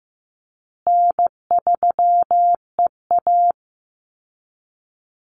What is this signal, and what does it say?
Morse code, heard as a keyed tone, about 15 wpm, 700 Hz. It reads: N3EA
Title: Easy Call Signs Repeated 2x - 15wpm